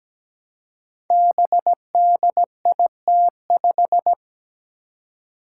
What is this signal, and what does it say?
Morse code, heard as a keyed tone, about 17 wpm, 700 Hz. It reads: BDIT5
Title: Five Challenging Letter-Number Combinations 17wpm